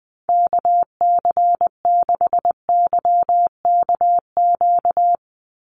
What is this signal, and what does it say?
Morse code, heard as a keyed tone, about 19 wpm, 700 Hz. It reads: KC6YKQ
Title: USA Call Signs 20wpm